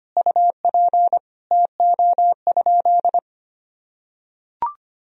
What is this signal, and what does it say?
Morse code, heard as a keyed tone, about 25 wpm, 700 Hz. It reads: UP TO?
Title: Sentences from Top 100 Words and Abbreviations 25wpm